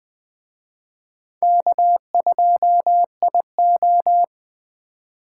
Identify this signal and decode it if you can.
Morse code, heard as a keyed tone, about 24 wpm, 700 Hz. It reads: K2IO
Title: Speed Racing - USA 1x2 Call Signs 20wpm